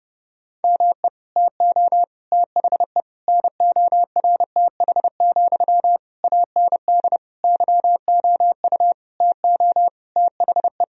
Morse code, heard as a keyed tone, about 30 wpm, 700 Hz. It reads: ME TO THE NORTH, AND YOU TO THE
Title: Sentences from Top 1000 Words - Review Part 2 - 30wpm